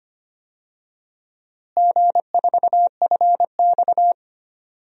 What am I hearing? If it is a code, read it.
Morse code, heard as a keyed tone, about 25 wpm, 700 Hz. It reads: G4FX
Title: DXCC-100 Call Signs - 4 Characters - Repeated 2x - 25wpm